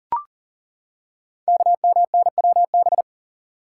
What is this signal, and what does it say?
Morse code, heard as a keyed tone, about 40 wpm, 700 Hz. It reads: KMNWB
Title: Code Groups - 5 Letters - 40wpm